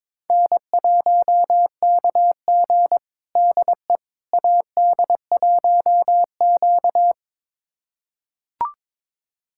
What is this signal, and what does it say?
Morse code, heard as a keyed tone, about 22 wpm, 700 Hz. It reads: N1KG DE AD1Q
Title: USA Callsign DE Callsign - 4 Characters - 22wpm